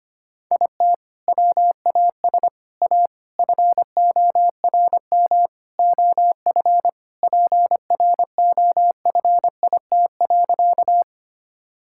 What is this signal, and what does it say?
Morse code, heard as a keyed tone, about 25 wpm, 700 Hz. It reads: IT WAS A FORM OF PROFIT.
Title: Sentences from Top 1000 Words - Review Part 5 - 25wpm